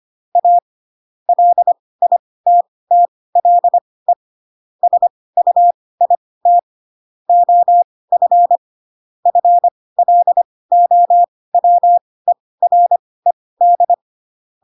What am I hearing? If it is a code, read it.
Morse code, heard as a keyed tone, about 25 wpm, spaced as for 18 wpm, 700 Hz. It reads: A LITTLE SUIT OF FLOWERED